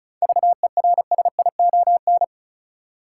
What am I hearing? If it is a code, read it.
Morse code, heard as a keyed tone, about 36 wpm, 700 Hz. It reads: VERSION
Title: Seven Letter Words - 35wpm